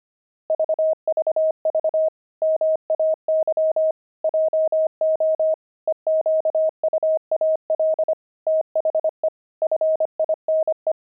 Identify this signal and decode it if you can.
Morse code, heard as a keyed tone, about 25 wpm, 630 Hz. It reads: VVV MAY JO EQUAL THE FINE
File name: pangram-fs-25-cs-25-puzzles